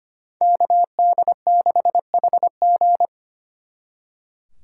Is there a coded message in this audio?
Yes: Morse code from 0.4 to 3.1 s, heard as a keyed tone, about 32 wpm, 700 Hz. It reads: KD6HG